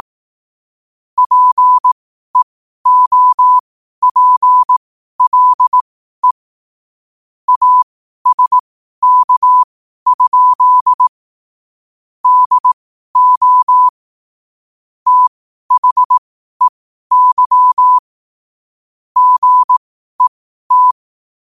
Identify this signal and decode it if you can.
Morse code, heard as a keyed tone, about 18 wpm, spaced as for 12 wpm, 1000 Hz. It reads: PEOPLE ASK? DO THEY GET
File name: qotdfiles / 12wpm